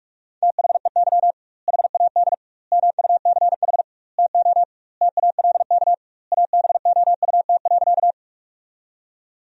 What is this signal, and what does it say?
Morse code, heard as a keyed tone, about 46 wpm, 700 Hz. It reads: THEY HAD MUCH TO TALK ABOUT.
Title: 2x Word-Spacing - Sentences from Top 400 Words - 45wpm